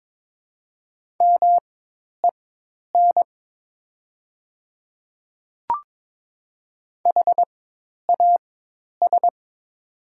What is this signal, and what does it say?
Morse code, heard as a keyed tone, about 22 wpm, spaced as for 10 wpm, 700 Hz. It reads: MEN HAS
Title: 4x Character Spacing - Top 500 Words - 22wpm